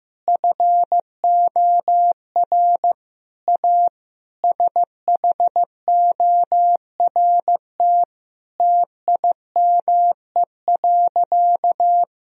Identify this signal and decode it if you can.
Morse code, heard as a keyed tone, about 15 wpm, 700 Hz. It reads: FOR A SHORT TIME.
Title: Sentences from Top 1000 Words - Review Part 2 - 15wpm